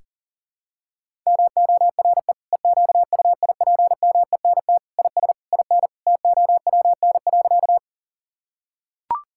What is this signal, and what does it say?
Morse code, heard as a keyed tone, about 37 wpm, 700 Hz. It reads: MORE EQUIPMENT IS IN TOWN.
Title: Sentences from Top 1000 Words - Review Part 5 - 40wpm